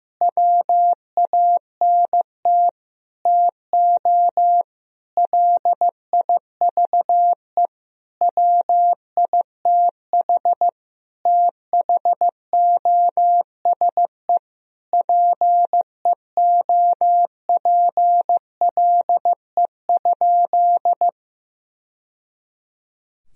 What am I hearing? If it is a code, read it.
Morse code, heard as a keyed tone, about 15 wpm, 700 Hz. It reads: WANT TO LIVE WITH THOSE PEOPLE?
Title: Sentences from Top 1000 Words - Review Part 1 - 15wpm